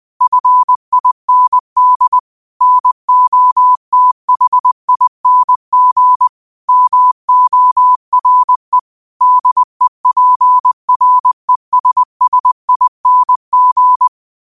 Morse code, heard as a keyed tone, about 20 wpm, 1000 Hz. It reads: FIND NOTHING MORE DEPRESSING